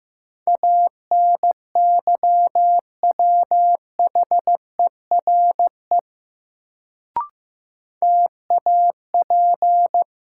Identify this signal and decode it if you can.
Morse code, heard as a keyed tone, about 15 wpm, 700 Hz. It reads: ANYWHERE TAP
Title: Sentences from Top 1400 Words - New Word Drill - 15wpm